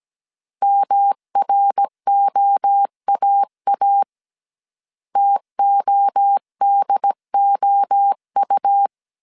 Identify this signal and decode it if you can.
Morse code, heard as a keyed tone, about 17 wpm, 800 Hz. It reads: MROAA TODOU